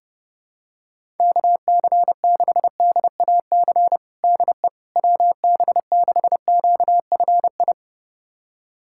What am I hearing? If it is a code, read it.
Morse code, heard as a keyed tone, about 32 wpm, 700 Hz. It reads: KC6DAC DE WB6QFI
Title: USA Callsign DE Callsign - 6 Characters - 30wpm